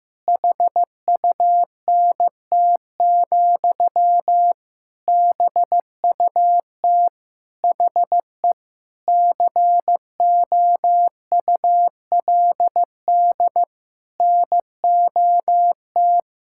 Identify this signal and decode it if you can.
Morse code, heard as a keyed tone, about 15 wpm, 700 Hz. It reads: HUNT, BUT HE COULD NOT
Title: Sentences from Top 1600 Words 15wpm